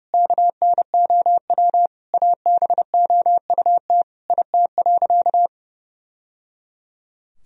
Morse code, heard as a keyed tone, about 30 wpm, 700 Hz. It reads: KNOW ABOUT IT.
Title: Sentences from Top 1000 Words - Review Part 1 - 30wpm